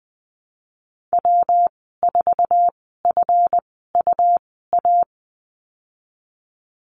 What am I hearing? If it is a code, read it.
Morse code, heard as a keyed tone, about 20 wpm, spaced as for 15 wpm, 700 Hz. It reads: W4FUA
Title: 147 Lesson 24g - Letter y - Call Signs 20wpm